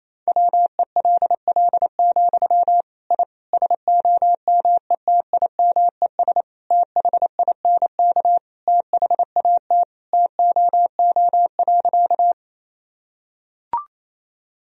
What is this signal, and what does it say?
Morse code, heard as a keyed tone, about 28 wpm, 700 Hz. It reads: WELL, I SOMETIMES THINK THAT TOO.
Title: Sentences from Top 1000 Words - Review Part 3 - 28wpm